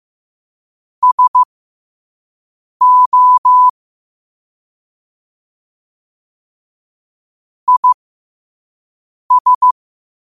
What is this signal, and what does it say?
Morse code, heard as a keyed tone, about 15 wpm, spaced as for 5 wpm, 1000 Hz. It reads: SO IS
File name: qotdfiles / 5wpm